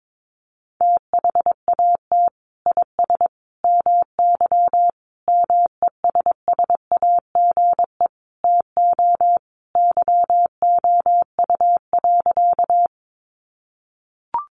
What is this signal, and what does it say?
Morse code, heard as a keyed tone, about 22 wpm, 700 Hz. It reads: THAT IS MY MESSAGE TO YOU.
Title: Sentences from Top 1100 Words 22wpm